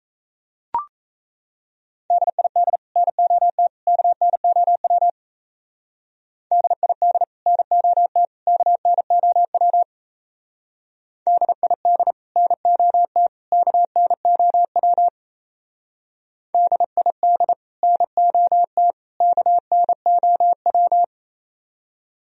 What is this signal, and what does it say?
Morse code, heard as a keyed tone, about 32 wpm, 700 Hz. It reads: DID NOT KNOW DID NOT KNOW DID NOT KNOW DID NOT KNOW
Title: Speed-Racing - Sets of 3 Words 28wpm